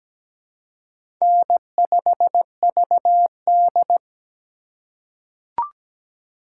Morse code, heard as a keyed tone, about 17 wpm, 700 Hz. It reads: N5VD
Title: USA Call Signs - 2x Character Spacing - 17wpm